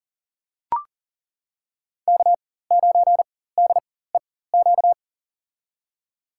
Morse code, heard as a keyed tone, about 40 wpm, spaced as for 19 wpm, 700 Hz. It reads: K9DEQ
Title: USA Call Signs - 5 Characters - 4x Character Spacing - 40wpm